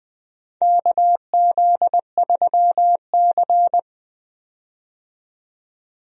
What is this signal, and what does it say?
Morse code, heard as a keyed tone, about 20 wpm, 700 Hz. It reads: KZ3C
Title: USA Call Signs - 4 Characters - 20wpm